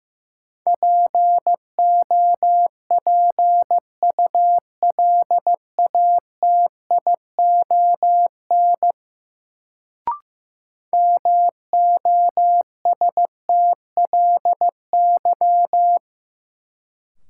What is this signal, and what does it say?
Morse code, heard as a keyed tone, about 15 wpm, 700 Hz. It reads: POPULATION MOSTLY
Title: Sentences from Top 1300 Words - New Word Drill Alt - 15wpm